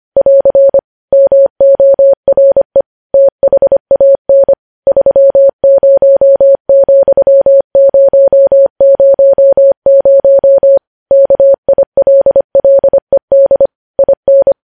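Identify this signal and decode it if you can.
Morse code, heard as a keyed tone, about 25 wpm, 550 Hz. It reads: <AR> MORE THAN 30,000 KILLED IN
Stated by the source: Morse news headlines sent at 25 wpm.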